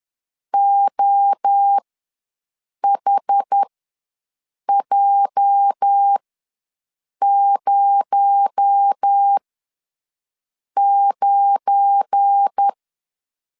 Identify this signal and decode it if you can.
Morse code, heard as a keyed tone, about 10 wpm, spaced as for 6 wpm, 800 Hz. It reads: OHJ09
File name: LEZIONI / 8 YJ45 / 8 YJ ws 10wpm 4043